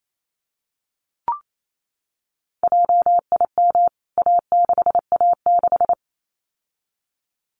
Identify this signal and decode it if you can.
Morse code, heard as a keyed tone, about 28 wpm, 700 Hz. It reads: JIM A6A6
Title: CWT Exchange 28wpm